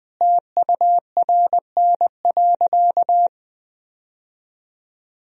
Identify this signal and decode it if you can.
Morse code, heard as a keyed tone, about 20 wpm, 700 Hz. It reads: TURN.
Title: Sentences from Top 1000 Words - Review Part 1 - 20wpm